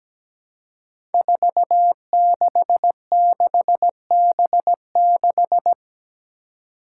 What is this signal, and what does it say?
Morse code, heard as a keyed tone, about 17 wpm, 700 Hz. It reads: 466B6
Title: Five Challenging Letter-Number Combinations 17wpm